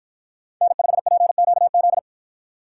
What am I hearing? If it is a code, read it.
Morse code, heard as a keyed tone, about 49 wpm, 700 Hz. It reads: N5PQZ